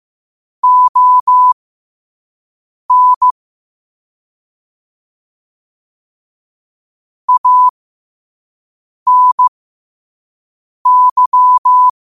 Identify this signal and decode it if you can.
Morse code, heard as a keyed tone, about 15 wpm, spaced as for 5 wpm, 1000 Hz. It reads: ON ANY
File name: qotdfiles / 5wpm